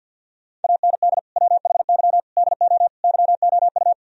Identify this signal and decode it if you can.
Morse code, heard as a keyed tone, about 50 wpm, 700 Hz. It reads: AND WHY DO YOU
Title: Sentences from Top 1000 Words - Review Part 2 - 50wpm